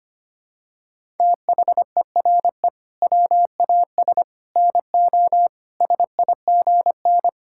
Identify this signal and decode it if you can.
Morse code, heard as a keyed tone, about 25 wpm, 700 Hz. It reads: THERE WAS NO SIGN
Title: Sentences from Top 1000 Words - Review Part 6 - 25wpm